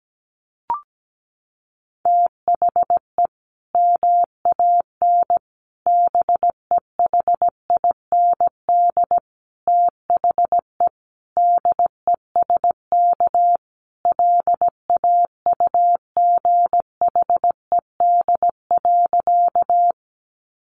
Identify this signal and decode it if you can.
Morse code, heard as a keyed tone, about 17 wpm, 700 Hz. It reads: THE MAN BEHIND THE DESK LAUGHED.
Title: A Sound Of Thunder 17wpm